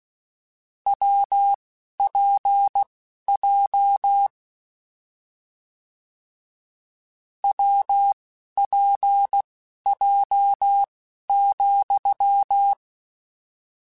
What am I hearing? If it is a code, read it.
Morse code, heard as a keyed tone, about 16 wpm, spaced as for 11 wpm, 800 Hz. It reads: WPJ WPJ,
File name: Difficult Combos / Difficult Combos-mp3s / WPJ WPJ1 @10